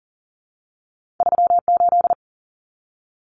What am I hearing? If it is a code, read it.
Morse code, heard as a keyed tone, about 40 wpm, 700 Hz. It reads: 38
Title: Numbers - 2-Digit 40wpm